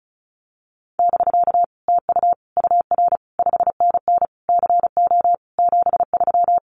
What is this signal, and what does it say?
Morse code, heard as a keyed tone, about 34 wpm, 700 Hz. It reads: <BK> TU UR 5NN CO 73
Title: Common QSO Phrases Encore 35wpm